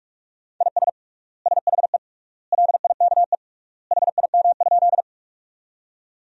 Morse code, heard as a keyed tone, about 46 wpm, 700 Hz. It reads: IS SHE LIKE HIM?
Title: 3x Word-Spacing - Sentences from Top 100 Words - 45wpm